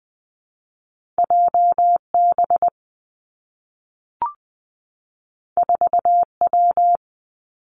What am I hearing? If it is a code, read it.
Morse code, heard as a keyed tone, about 20 wpm, 700 Hz. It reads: J B 4 W
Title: Two Challenging Letter-Number Combinations 20wpm